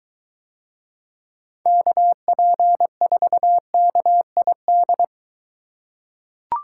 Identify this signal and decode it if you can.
Morse code, heard as a keyed tone, about 23 wpm, 700 Hz. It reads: KP4KID